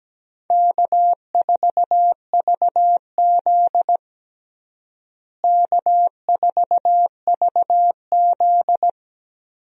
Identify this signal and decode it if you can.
Morse code, heard as a keyed tone, about 17 wpm, 700 Hz. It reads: K4VZ K4VZ
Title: USA Call Signs - 4 Characters - Repeated 3x - 17wpm